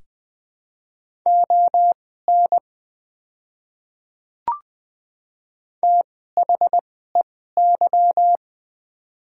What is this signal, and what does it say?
Morse code, heard as a keyed tone, about 20 wpm, spaced as for 14 wpm, 700 Hz. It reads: ON THEY
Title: 2x Character Spacing - Top 200 Words - 20wpm